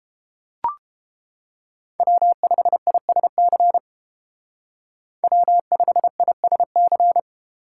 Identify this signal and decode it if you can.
Morse code, heard as a keyed tone, about 30 wpm, 700 Hz. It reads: W5ISC W5ISC